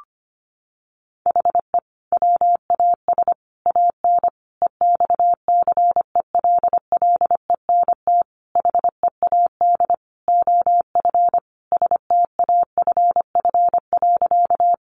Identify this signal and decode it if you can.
Morse code, heard as a keyed tone, about 25 wpm, 700 Hz. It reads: HE WAS AN EXCELLENT HEAD OF STAFF.
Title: Sentences from Top 1200 Words 25wpm